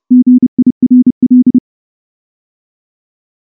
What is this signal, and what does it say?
Morse code, heard as a keyed tone, about 28 wpm, 250 Hz. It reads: GIRL
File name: morse / morse-training-30WPM